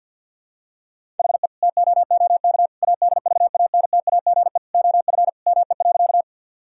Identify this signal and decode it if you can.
Morse code, heard as a keyed tone, about 50 wpm, 700 Hz. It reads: HE TOOK ADVANTAGE OF ME.
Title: Sentences from Top 1000 Words - Review Part 6 - 50wpm